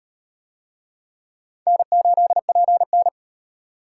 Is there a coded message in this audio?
Yes: Morse code from 1.7 to 3.1 s, heard as a keyed tone, about 36 wpm, 700 Hz. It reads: N8PN